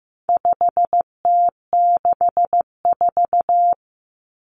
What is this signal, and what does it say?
Morse code, heard as a keyed tone, about 15 wpm, 700 Hz. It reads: 5T64
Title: Common Cut Numbers - 4-Digit 15wpm